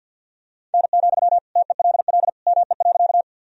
Morse code, heard as a keyed tone, about 48 wpm, 700 Hz. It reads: N, TELL ME.
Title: Sentences from Top 200 Words and Abbreviations 50wpm